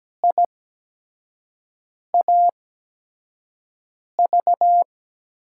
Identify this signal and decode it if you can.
Morse code, heard as a keyed tone, about 17 wpm, spaced as for 5 wpm, 700 Hz. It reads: IAV
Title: USA Call Signs - 5 Characters - 8x Character Spacing - 17wpm